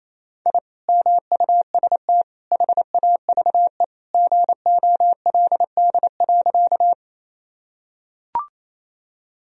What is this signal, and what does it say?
Morse code, heard as a keyed tone, about 28 wpm, 700 Hz. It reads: I MUST HAVE GOLD.
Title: Sentences from Top 1000 Words - Review Part 3 - 28wpm